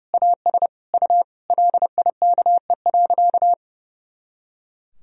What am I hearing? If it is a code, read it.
Morse code, heard as a keyed tone, about 30 wpm, 700 Hz. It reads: AS U LIKE.
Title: Sentences from Top 500 Words and Abbreviations 30wpm